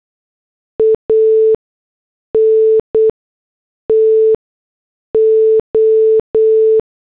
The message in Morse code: ANTO